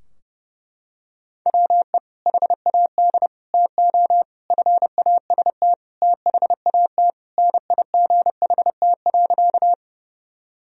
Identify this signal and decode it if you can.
Morse code, heard as a keyed tone, about 32 wpm, 700 Hz. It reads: WE HAD TO FAST THAT NIGHT.
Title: Sentences from Top 1000 Words - Review Part 2 - 30wpm